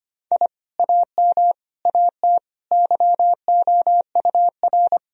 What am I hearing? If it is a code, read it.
Morse code, heard as a keyed tone, about 25 wpm, 700 Hz. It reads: I AM AT YOUR
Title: Sentences from Top 1000 Words - Review Part 4 - 25wpm